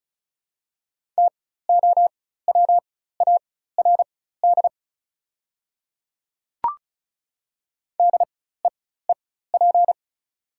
Morse code, heard as a keyed tone, about 36 wpm, spaced as for 16 wpm, 700 Hz. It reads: TOWARD DEEP
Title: 4x Character Spacing - Top 500 Words - 35wpm